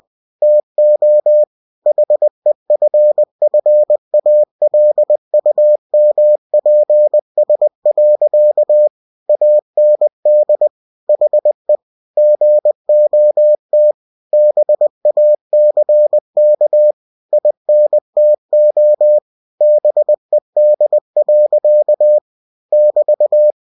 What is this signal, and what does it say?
Morse code, heard as a keyed tone, about 20 wpm, 600 Hz. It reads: TO HEFFALUMPS. AND HE GOT BACK INTO BED. <BT>
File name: cw-books / winnie-the-pooh / winnie-20wpm-page-0039